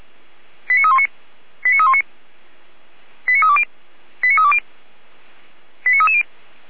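Five-tone selective call (ZVEI), heard as quick runs of five tones, tones of about 70 ms. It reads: 89319 89319 89320 89320 89330
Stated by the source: ZVEI five-tone call 89310 received by SDR.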